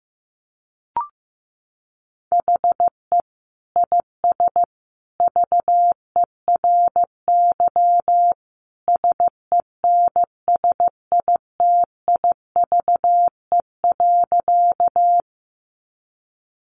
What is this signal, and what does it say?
Morse code, heard as a keyed tone, about 15 wpm, 700 Hz. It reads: HE IS VERY SENSITIVE.
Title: Sentences from Top 1400 Words 15wpm